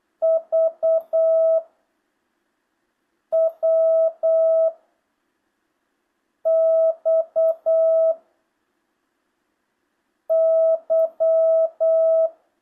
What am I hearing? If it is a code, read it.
Morse code, heard as a keyed tone, about 8 wpm, 650 Hz. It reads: V W X Y
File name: jh8cbh / 2025cw